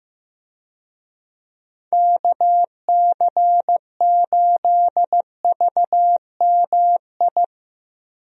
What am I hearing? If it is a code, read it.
Morse code, heard as a keyed tone, about 16 wpm, 700 Hz. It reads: KC8VMI
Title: USA Call Signs - 6 Characters - Repeated 3x - 15wpm